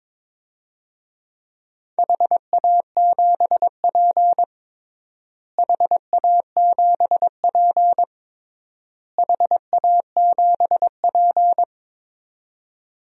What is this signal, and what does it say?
Morse code, heard as a keyed tone, about 22 wpm, 700 Hz. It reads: HA7P HA7P HA7P
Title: DXCC-100 Call Signs - 4 Characters - Repeated 3x - 22wpm